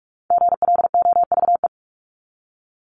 Morse code, heard as a keyed tone, about 44 wpm, 700 Hz. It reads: GLOVE
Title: Top 1751-2000 Words 45wpm